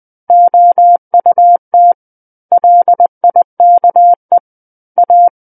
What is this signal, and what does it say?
Morse code, heard as a keyed tone, about 20 wpm, 700 Hz. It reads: OUT LIKE A
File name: 20 WPM